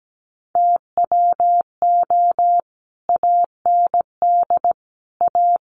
Morse code, heard as a keyed tone, about 17 wpm, 700 Hz. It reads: TWO AND A
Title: Sets of 5 Words 17wpm